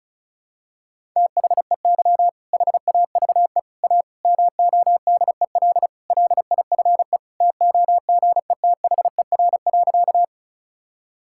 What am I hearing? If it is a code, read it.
Morse code, heard as a keyed tone, about 36 wpm, 700 Hz. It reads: THEY HAVE A MODEL LIFE TOGETHER.
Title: Sentences from Top 1000 Words - Review Part 5 - 35wpm